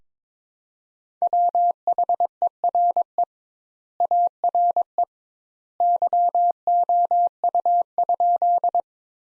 Morse code, heard as a keyed tone, about 22 wpm, 700 Hz. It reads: WHERE ARE YOU?
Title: Sentences Top 200 Words - 22wpm